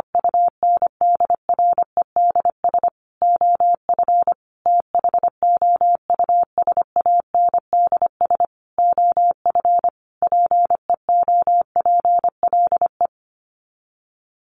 Morse code, heard as a keyed tone, about 25 wpm, 700 Hz. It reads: UNDREDS OF THOUSANDS OF PEOPLE
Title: Sets of 5 Words Encore2 25wpm